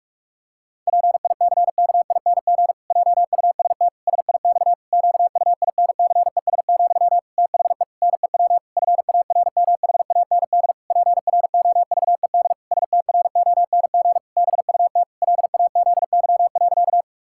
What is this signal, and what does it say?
Morse code, heard as a keyed tone, about 41 wpm, 700 Hz. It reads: PICKING JUST SIX QUINCES, THE NEW FARMHAND PROVED STRONG BUT LAZY.
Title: Pangram Challenge v2 - 45wpm